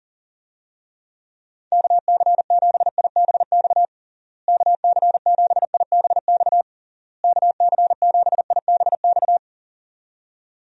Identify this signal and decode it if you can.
Morse code, heard as a keyed tone, about 40 wpm, 700 Hz. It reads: KC7IBX KC7IBX KC7IBX
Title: USA Call Signs - 6 Characters - Repeated 3x - 40wpm